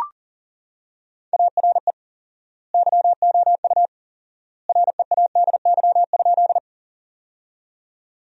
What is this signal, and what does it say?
Morse code, heard as a keyed tone, about 40 wpm, 700 Hz. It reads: ARE YOU READY?
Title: 4x Word-Spacing - Sentences from Top 400 Words - 40wpm